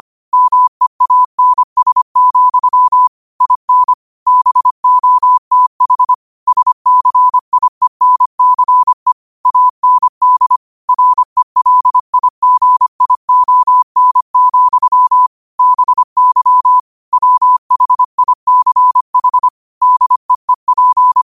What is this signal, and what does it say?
Morse code, heard as a keyed tone, about 25 wpm, 1000 Hz. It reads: MEANS, IN BOTH SCIENCE AND RELIGION, BY WHICH DEEP